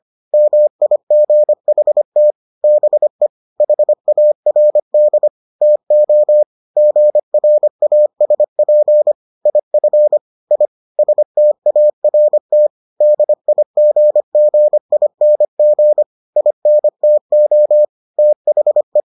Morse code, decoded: MIGHT BE HARD TO GRASP IF I START DIGGING INTO THE

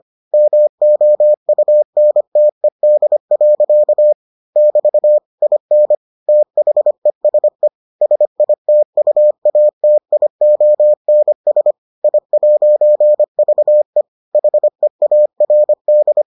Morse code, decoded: MOUNTED. <BT> IN THESE SITUATIONS I'VE HEARD

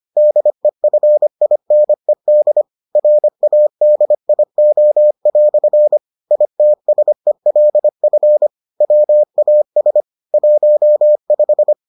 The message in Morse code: DEFINED RADIO" ITSELF WAS 15

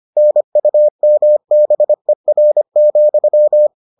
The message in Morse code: NUMBER,

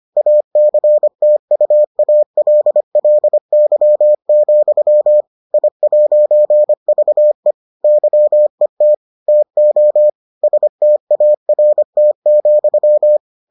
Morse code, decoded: ACTUALLY, I'VE YET TO START,

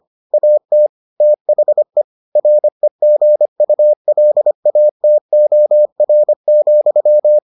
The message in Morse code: AT THE REGULATOR,

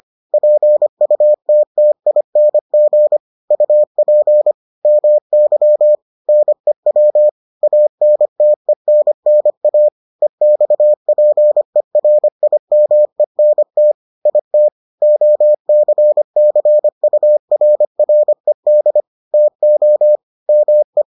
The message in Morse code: PUTTING UP MY NEW ANTENNA EXPERIMENT IT OCCURRED TO ME

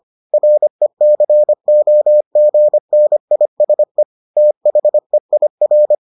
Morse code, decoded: RECOGNISE THEIR

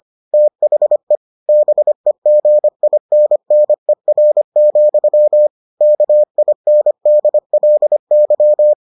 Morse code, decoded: THE BEGINNER, KINDLY

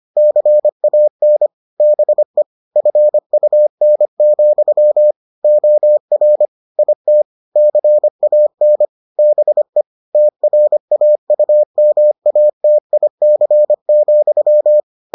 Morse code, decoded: CAN BE FUN, OR IT CAN BE TRAUMATIC,

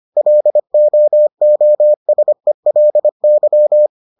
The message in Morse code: LOOSELY

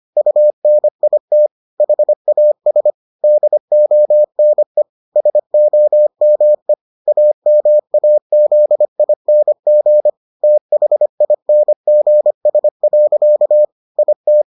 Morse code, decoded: UNIT HAS DONE SOME AMAZING THINGS. IT

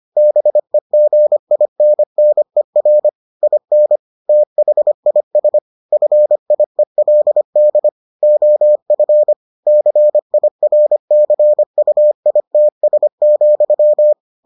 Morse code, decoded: BEGINNER IN THIS FIELD OF CIRCUITS,

